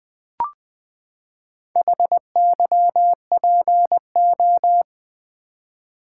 Morse code, heard as a keyed tone, about 20 wpm, 700 Hz. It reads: HYPO
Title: Build A Word - 4 Letters At A Time - 10-Letter Words - 20wpm